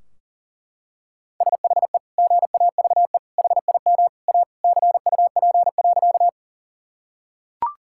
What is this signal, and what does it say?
Morse code, heard as a keyed tone, about 40 wpm, 700 Hz. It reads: SHE GAVE HIM A CUP.
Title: Sentences from Top 1600 Words 40wpm